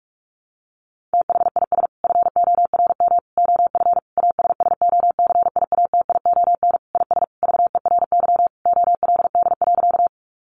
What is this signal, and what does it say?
Morse code, heard as a keyed tone, about 44 wpm, 700 Hz. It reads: THIS FORM OF ASSOCIATION IS VERY OLD.
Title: Sentences from Top 1100 Words 45wpm